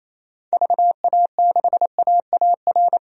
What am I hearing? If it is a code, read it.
Morse code, heard as a keyed tone, about 28 wpm, 700 Hz. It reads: VA6AAR
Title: DXCC-100 Call Signs - 6 Characters - 28wpm